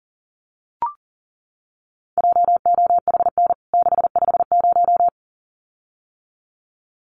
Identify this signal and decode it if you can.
Morse code, heard as a keyed tone, about 37 wpm, 700 Hz. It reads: JOHN 650
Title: CWT Exchange 40wpm